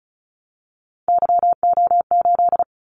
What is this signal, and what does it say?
Morse code, heard as a keyed tone, about 36 wpm, 700 Hz. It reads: YO8
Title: Callsign Prefixes for Top 25 DXCC - 35wpm